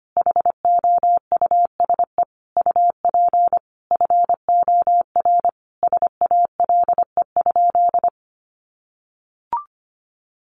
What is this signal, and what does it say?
Morse code, heard as a keyed tone, about 25 wpm, 700 Hz. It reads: HOUSE UP FOR SALE?
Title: Sentences from Top 1400 Words 25wpm